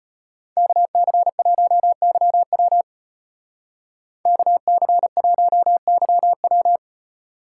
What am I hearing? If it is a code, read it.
Morse code, heard as a keyed tone, about 30 wpm, 700 Hz. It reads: KC1YW KC1YW
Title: Speed Racing - USA Call Signs 25wpm